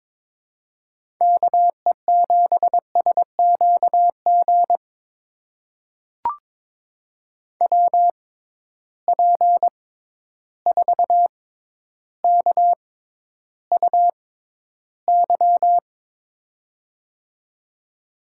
Morse code, heard as a keyed tone, about 22 wpm, spaced as for 11 wpm, 700 Hz. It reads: KE7SQG WP4KUY
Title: USA Call Signs - 6 Characters - 6x Character Spacing - 22wpm